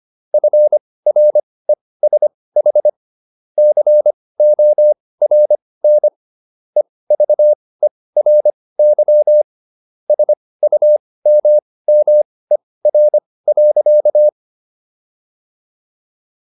Morse code, heard as a keyed tone, about 25 wpm, spaced as for 18 wpm, 600 Hz. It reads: FRESH CORN EVERY SUMMER.